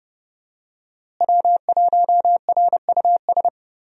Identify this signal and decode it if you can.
Morse code, heard as a keyed tone, about 30 wpm, 700 Hz. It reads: W1RUS
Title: USA Call Signs - 5 Characters - Repeated 2x - 30wpm